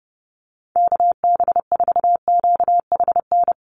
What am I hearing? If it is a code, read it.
Morse code, heard as a keyed tone, about 30 wpm, 700 Hz. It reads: KB4QHN
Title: USA Call Signs Repeated 3x 30wpm